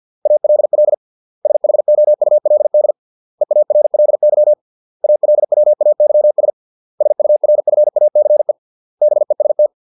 Morse code, heard as a keyed tone, about 50 wpm, 600 Hz. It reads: ALL SHOULD EARLY ALWAYS SURFACE BEST